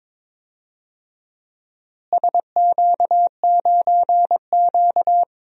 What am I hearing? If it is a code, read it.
Morse code, heard as a keyed tone, about 22 wpm, 700 Hz. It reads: SQ9Q
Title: DXCC-100 Call Signs - 4 Characters - Repeated 3x - 22wpm